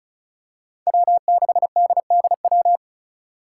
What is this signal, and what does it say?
Morse code, heard as a keyed tone, about 36 wpm, 700 Hz. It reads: W6DDW
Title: Five Challenging Letter-Number Combinations 35wpm